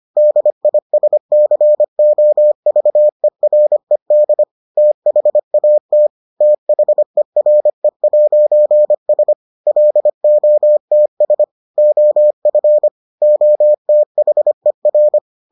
Morse code, heard as a keyed tone, about 25 wpm, 600 Hz. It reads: DISCOVERED THAT THERE'S LOTS OF OTHER